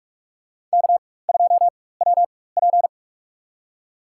Morse code, heard as a keyed tone, about 44 wpm, spaced as for 21 wpm, 690 Hz. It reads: K2WP
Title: USA Call Signs - 4 Characters - 4x Character Spacing - 45wpm